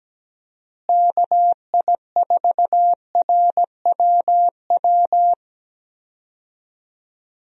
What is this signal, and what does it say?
Morse code, heard as a keyed tone, about 17 wpm, 700 Hz. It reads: KI4RWW
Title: USA Call Signs - 6 Characters - Repeated 2x - 17wpm